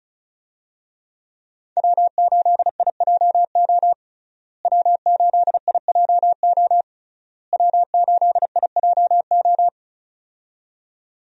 Morse code, heard as a keyed tone, about 36 wpm, 700 Hz. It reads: W8IJO W8IJO W8IJO
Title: USA Call Signs - 5 Characters - Repeated 3x - 35wpm